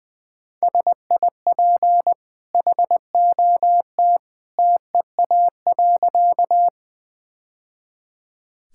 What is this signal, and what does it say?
Morse code, heard as a keyed tone, about 20 wpm, 700 Hz. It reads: SIP HOT TEA.
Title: Easy Sentences with Hints - 20wpm